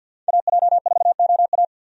Morse code, heard as a keyed tone, about 50 wpm, 700 Hz. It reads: AJ4OA
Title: USA Call Signs - 5 Characters - 50wpm